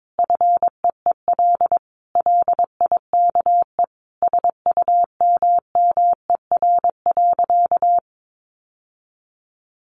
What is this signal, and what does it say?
Morse code, heard as a keyed tone, about 22 wpm, 700 Hz. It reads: FEEL LIKE SUMMER.
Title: Sentences from Top 1100 Words 22wpm